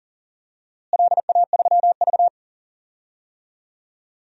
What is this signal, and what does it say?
Morse code, heard as a keyed tone, about 40 wpm, 700 Hz. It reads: LA3V
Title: DXCC-100 Call Signs - 4 Characters - Repeated 2x - 40wpm